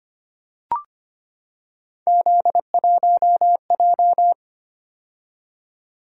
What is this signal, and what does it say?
Morse code, heard as a keyed tone, about 25 wpm, 700 Hz. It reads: Z1J
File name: Three Letter-Number Combinations - 25wpm